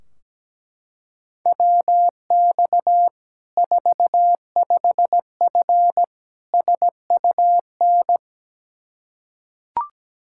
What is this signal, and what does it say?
Morse code, heard as a keyed tone, about 17 wpm, 700 Hz. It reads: WX 45F SUN
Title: K1USN SST QSO Practice 17wpm